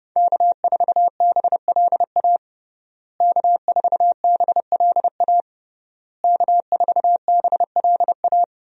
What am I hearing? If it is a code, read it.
Morse code, heard as a keyed tone, about 30 wpm, 700 Hz. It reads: K4BLA K4BLA K4BLA
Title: USA Call Signs - 5 Characters - Repeated 3x - 30wpm